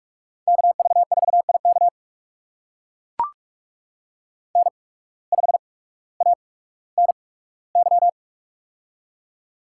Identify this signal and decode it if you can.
Morse code, heard as a keyed tone, about 44 wpm, spaced as for 19 wpm, 700 Hz. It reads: KV4IK N5ANY
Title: USA Call Signs - 5 Characters - 8x Character Spacing - 45wpm